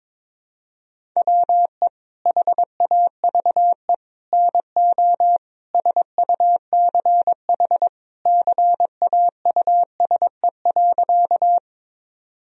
Morse code, heard as a keyed tone, about 22 wpm, 700 Hz. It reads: WE HAVE NO SUCH CAUSE.
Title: Sentences from Top 1000 Words - Review Part 1 - 22wpm